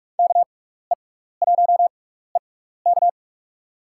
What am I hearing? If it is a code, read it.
Morse code, heard as a keyed tone, about 44 wpm, spaced as for 15 wpm, 700 Hz. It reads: KE1EK
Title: USA Call Signs - 5 Characters - 6x Character Spacing - 45wpm